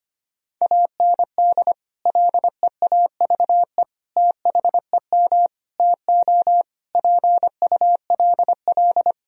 Morse code, heard as a keyed tone, about 25 wpm, 700 Hz. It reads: AND LEAVE THEM TO PULL
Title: Sentences from Top 1000 Words - Review Part 2 - 25wpm